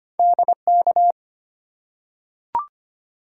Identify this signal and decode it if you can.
Morse code, heard as a keyed tone, about 25 wpm, 700 Hz. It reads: DK
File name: Two Letter-Number Combinations - 25wpm